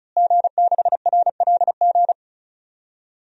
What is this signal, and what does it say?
Morse code, heard as a keyed tone, about 36 wpm, 690 Hz. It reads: G6RLG
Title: DXCC-100 Call Signs - 5 Characters - Repeated 3x - 35wpm